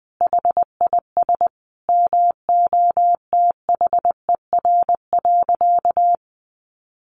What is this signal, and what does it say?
Morse code, heard as a keyed tone, about 20 wpm, 700 Hz. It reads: HIS MOTHER.
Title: Sentences from Top 700 Words 20wpm